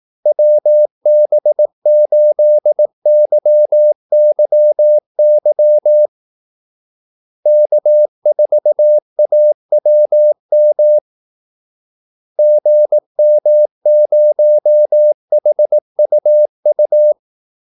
Morse code, heard as a keyed tone, about 18 wpm, 600 Hz. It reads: WB8YYY K4AWM GM0HUU
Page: Kaldesignaler 18 wpm